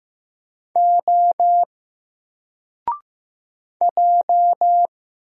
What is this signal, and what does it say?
Morse code, heard as a keyed tone, about 15 wpm, 700 Hz. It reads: O J
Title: Single Letter-Number 15wpm